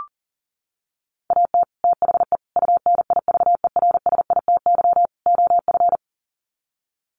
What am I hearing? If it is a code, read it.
Morse code, heard as a keyed tone, about 40 wpm, 700 Hz. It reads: AT THE UNIVERSITY OF
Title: Sets of 4 Words 40wpm